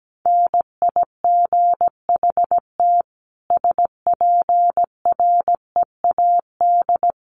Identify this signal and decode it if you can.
Morse code, heard as a keyed tone, about 17 wpm, 700 Hz. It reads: NIGHT SPREAD
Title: Sentences from Top 1200 Words 17wpm